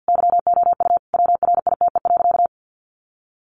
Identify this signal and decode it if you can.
Morse code, heard as a keyed tone, about 50 wpm, 700 Hz. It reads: YOU WRITE.
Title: Sentences from Top 800 Words 50wpm